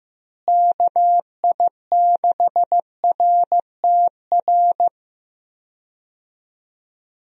Morse code, heard as a keyed tone, about 15 wpm, 700 Hz. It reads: KI6RTR
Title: USA Call Signs - 6 Characters - Repeated 2x - 15wpm